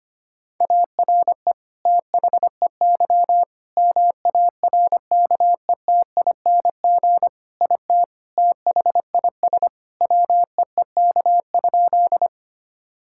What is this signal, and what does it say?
Morse code, heard as a keyed tone, about 25 wpm, 700 Hz. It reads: ARE THEY MARKETING IT THIS WEEK?
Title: Sentences from Top 1000 Words - Review Part 5 - 25wpm